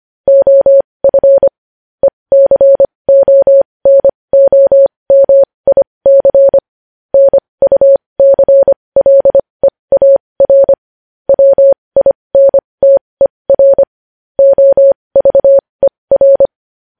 Morse code, heard as a keyed tone, about 25 wpm, spaced as for 20 wpm, 550 Hz. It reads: OF ECONOMIC NUCLEAR WINTER OVER
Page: news headlines in morse code at 20 wpm